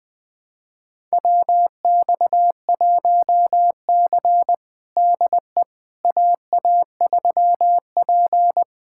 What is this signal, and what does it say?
Morse code, heard as a keyed tone, about 20 wpm, 700 Hz. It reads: WX1C DE AA3P
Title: USA Callsign DE Callsign - 4 Characters - 20wpm